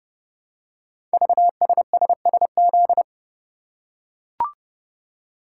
Morse code, heard as a keyed tone, about 30 wpm, 700 Hz. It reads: VSSSZ
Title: Code Groups - 5 Letters - 30wpm